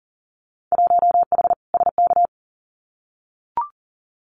Morse code, heard as a keyed tone, about 39 wpm, 700 Hz. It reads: 1H SK
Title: Winter Field Day Exchanges 40wpm